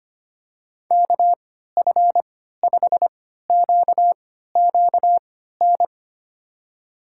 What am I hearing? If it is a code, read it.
Morse code, heard as a keyed tone, about 25 wpm, spaced as for 14 wpm, 700 Hz. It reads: KF5QQN
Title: USA Call Signs - 6 Characters - 3x Character Spacing - 25wpm